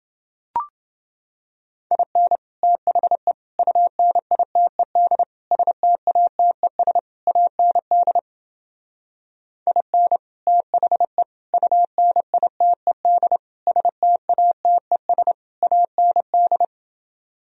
Morse code, heard as a keyed tone, about 29 wpm, 700 Hz. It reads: IN THE UNITED STATES AND IN THE UNITED STATES AND